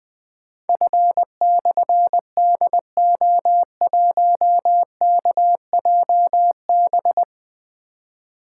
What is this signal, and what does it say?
Morse code, heard as a keyed tone, about 20 wpm, 700 Hz. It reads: F/DO1KJB
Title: Difficult Call Signs 20wpm